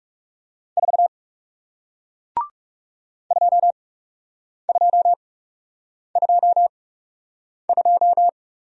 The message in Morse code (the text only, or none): 4 2222